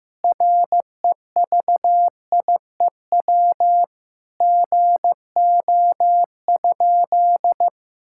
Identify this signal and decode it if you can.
Morse code, heard as a keyed tone, about 15 wpm, 700 Hz. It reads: REVIEW GO?
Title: Sentences from Top 1000 Words - Review Part 6 - 15wpm